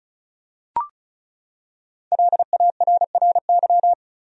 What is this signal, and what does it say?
Morse code, heard as a keyed tone, about 36 wpm, 700 Hz. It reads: LARRY
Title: USA Common Names - 35wpm